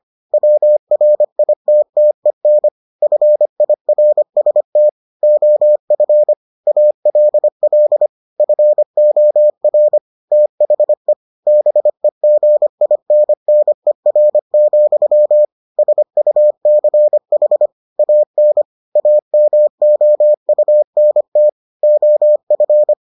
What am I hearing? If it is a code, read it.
Morse code, heard as a keyed tone, about 25 wpm, 600 Hz. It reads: WRITTEN FIRST OF ALL FOR THE BEGINNER, SUCH AN AMOUNT OF